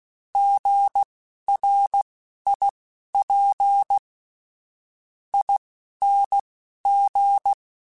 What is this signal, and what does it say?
Morse code, heard as a keyed tone, about 16 wpm, spaced as for 11 wpm, 800 Hz. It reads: GRIP ING
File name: Alphabetical / Alphabetical mp3s / Gs @10